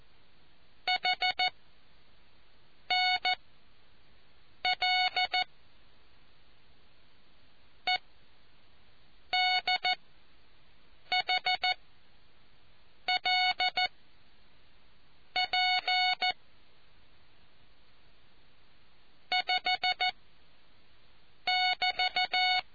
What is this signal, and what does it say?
Morse code, heard as a keyed tone, about 14 wpm, spaced as for 5 wpm, 740 Hz. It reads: HNL EDHLP 5<BT>